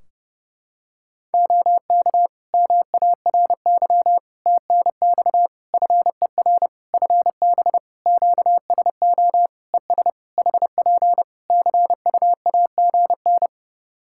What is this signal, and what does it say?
Morse code, heard as a keyed tone, about 30 wpm, 700 Hz. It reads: OK MARY TNX FER FB QSO ES HP CUAGN
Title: Practice QSO - Exchange 4 - Ending - 30wpm